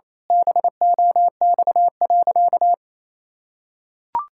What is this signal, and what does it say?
Morse code, heard as a keyed tone, about 28 wpm, 700 Hz. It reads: BOX.
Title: Pangram Challenge v2 - 28wpm